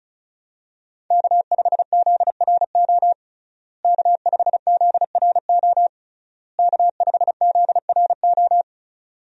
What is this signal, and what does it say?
Morse code, heard as a keyed tone, about 36 wpm, 700 Hz. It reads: K5ZRO K5ZRO K5ZRO
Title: USA Call Signs - 5 Characters - Repeated 3x - 35wpm